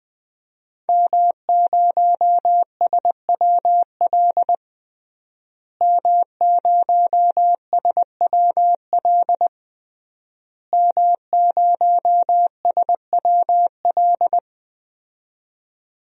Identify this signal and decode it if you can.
Morse code, heard as a keyed tone, about 21 wpm, 700 Hz. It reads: M0SWL M0SWL M0SWL
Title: DXCC-100 Call Signs - 5 Characters - Repeated 3x - 20wpm